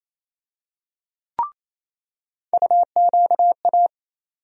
Morse code, heard as a keyed tone, about 27 wpm, 700 Hz. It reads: UQA
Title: USA Call Sign Postfixes - 3 Letters 28wpm